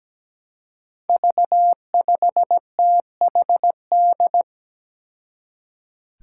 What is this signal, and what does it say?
Morse code, heard as a keyed tone, about 17 wpm, 700 Hz. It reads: V5THD
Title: Five Challenging Letter-Number Combinations 17wpm